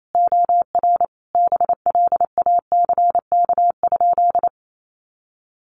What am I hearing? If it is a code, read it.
Morse code, heard as a keyed tone, about 28 wpm, 700 Hz. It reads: OR BLACK?
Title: Sentences from Top 400 Words 28wpm